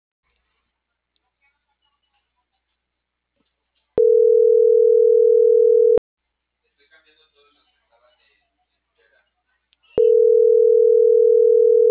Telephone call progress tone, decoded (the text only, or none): ringback tone